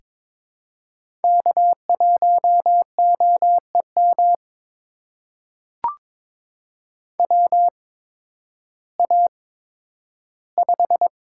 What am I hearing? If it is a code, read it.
Morse code, heard as a keyed tone, about 22 wpm, spaced as for 12 wpm, 700 Hz. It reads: K1OEM WA5
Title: USA Call Signs - 8x Character Spacing - 22wpm